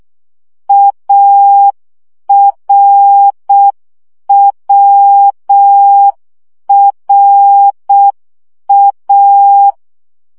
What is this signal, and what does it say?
Morse code, heard as a keyed tone, about 6 wpm, 800 Hz. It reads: ARWRA
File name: CW gyakorlas a r w 5 baud